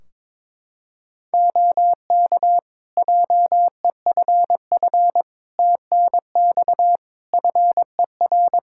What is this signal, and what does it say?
Morse code, heard as a keyed tone, about 22 wpm, 700 Hz. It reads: OK JEFF TNX FER
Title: Practice QSO - Exchange 4 - Ending - 22wpm